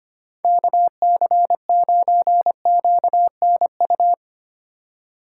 Morse code, heard as a keyed tone, about 25 wpm, 700 Hz. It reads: KC9QNU
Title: USA Call Signs - 6 Characters - 3x Character Spacing - 25wpm